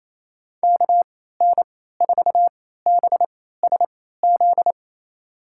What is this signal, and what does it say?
Morse code, heard as a keyed tone, about 28 wpm, spaced as for 16 wpm, 700 Hz. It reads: KN4BSZ
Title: USA Call Signs - 6 Characters - 3x Character Spacing - 28wpm